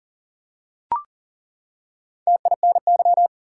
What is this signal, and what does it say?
Morse code, heard as a keyed tone, about 40 wpm, 700 Hz. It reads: TINY
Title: Four Letter Words - 40wpm